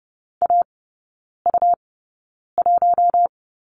Morse code, heard as a keyed tone, about 30 wpm, spaced as for 9 wpm, 700 Hz. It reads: AU1
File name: Lesson 17e - Letter u - Three Characters 30wpm